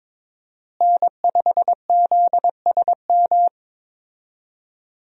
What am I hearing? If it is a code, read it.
Morse code, heard as a keyed tone, about 22 wpm, 700 Hz. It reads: N5ZSM
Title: USA Call Signs - 5 Characters - Repeated 3x - 22wpm